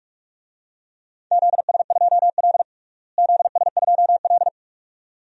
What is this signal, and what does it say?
Morse code, heard as a keyed tone, about 45 wpm, 700 Hz. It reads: ZS2L ZS2L
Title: DXCC-100 Call Signs - 4 Characters - Repeated 2x - 45wpm